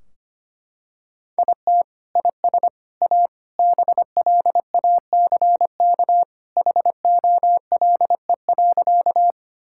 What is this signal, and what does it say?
Morse code, heard as a keyed tone, about 25 wpm, 700 Hz. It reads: IT IS A BLACK HOLE.
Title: Sentences from Top 1600 Words 25wpm